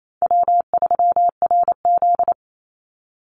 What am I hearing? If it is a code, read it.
Morse code, heard as a keyed tone, about 28 wpm, 700 Hz. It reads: W3RZ
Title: Easy Call Signs Repeated 3x - 28wpm